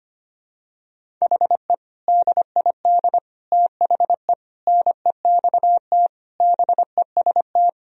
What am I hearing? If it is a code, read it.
Morse code, heard as a keyed tone, about 25 wpm, 700 Hz. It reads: HE DID THE NEXT BEST
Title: Sentences from Top 1000 Words - Review Part 1 - 25wpm